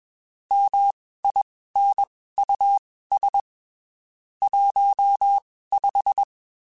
Morse code, heard as a keyed tone, about 21 wpm, spaced as for 15 wpm, 800 Hz. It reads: MINUS 15
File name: (spaced) / QSOs Spaced-mp3s / WD0FBT @20 sp